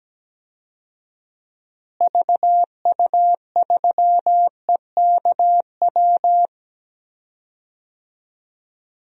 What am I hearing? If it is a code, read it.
Morse code, heard as a keyed tone, about 17 wpm, 700 Hz. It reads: VU3EKW
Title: DXCC-100 Call Signs - 6 Characters - Repeated 3x - 17wpm